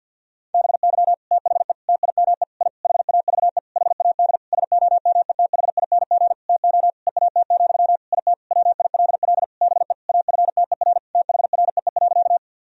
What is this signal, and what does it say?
Morse code, heard as a keyed tone, about 48 wpm, 700 Hz. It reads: BY THE TIME I HAVE HAD SOMETHING TO EAT, IT WILL BE AFTER THREE.
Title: Sentences from Top 1000 Words - Review Part 1 - 50wpm